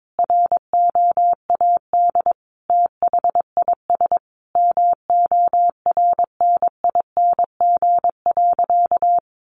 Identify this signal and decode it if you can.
Morse code, heard as a keyed tone, about 22 wpm, 700 Hz. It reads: ROAD THIS MORNING.
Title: Sentences from Top 1100 Words 22wpm